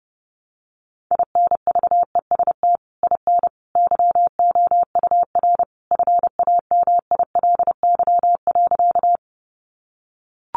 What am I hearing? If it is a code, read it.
Morse code, heard as a keyed tone, about 28 wpm, 700 Hz. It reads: INVEST IN YOUR FAMILY.
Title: Sentences from Top 1100 Words 30wpm